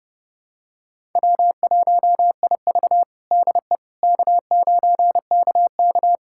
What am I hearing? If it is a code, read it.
Morse code, heard as a keyed tone, about 30 wpm, 700 Hz. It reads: W1IV DE K9KK
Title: USA Callsign DE Callsign - 4 Characters - 30wpm